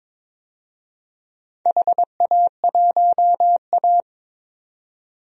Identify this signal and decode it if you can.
Morse code, heard as a keyed tone, about 22 wpm, 700 Hz. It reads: HA1A
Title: DXCC-100 Call Signs - 4 Characters - 22wpm